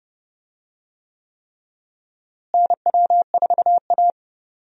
Morse code, heard as a keyed tone, about 32 wpm, 700 Hz. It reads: NW4A
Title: USA Call Signs - 4 Characters - Repeated 3x - 30wpm